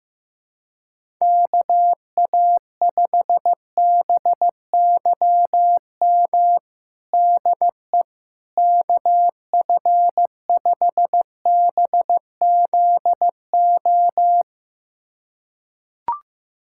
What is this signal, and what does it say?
Morse code, heard as a keyed tone, about 15 wpm, 700 Hz. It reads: KA5BYM DE KF5BZO
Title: USA Callsign DE Callsign 15wpm